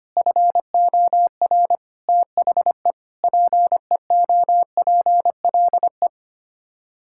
Morse code, heard as a keyed tone, about 25 wpm, 700 Hz. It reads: FOR THE PEOPLE
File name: Sets of 9 Words - 25wpm